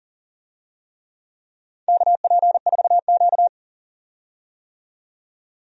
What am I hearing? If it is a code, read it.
Morse code, heard as a keyed tone, about 39 wpm, 700 Hz. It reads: KP4Q
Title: DXCC-100 Call Signs - 4 Characters - 40wpm